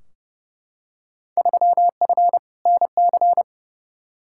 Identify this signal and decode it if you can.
Morse code, heard as a keyed tone, about 30 wpm, 700 Hz. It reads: 3F NC
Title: Field Day Exchanges - 30wpm